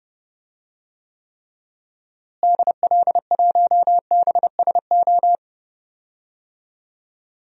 Morse code, heard as a keyed tone, about 30 wpm, 700 Hz. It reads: DL1BSO
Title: DXCC-100 Call Signs - 6 Characters - Repeated 2x - 30wpm